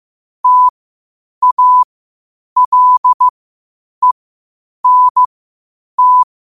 Morse code, heard as a keyed tone, about 15 wpm, spaced as for 8 wpm, 1000 Hz. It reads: TALENT
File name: 8wpm